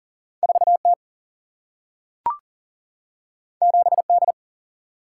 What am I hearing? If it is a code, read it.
Morse code, heard as a keyed tone, about 42 wpm, 700 Hz. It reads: 4 T 7 D
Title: Single-Letter Single-Number 40wpm